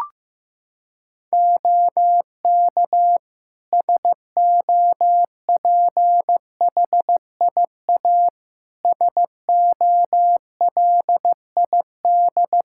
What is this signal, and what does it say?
Morse code, heard as a keyed tone, about 15 wpm, 700 Hz. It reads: OK SOPHIA SOLID
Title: Practice QSO - Exchange 3 - More Info - 15wpm